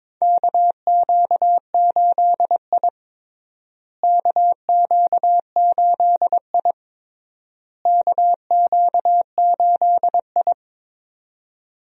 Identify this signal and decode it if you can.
Morse code, heard as a keyed tone, about 21 wpm, 700 Hz. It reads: KQ8I KQ8I KQ8I
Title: USA Call Signs - 4 Characters - Repeated 3x - 22wpm